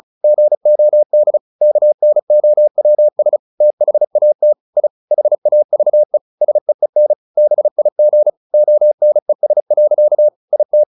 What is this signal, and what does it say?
Morse code, heard as a keyed tone, about 36 wpm, 600 Hz. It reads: GOD KNOWS THAT I HAVE SEEN BIG ONES. IT
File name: the old man and the sea 2 35wpm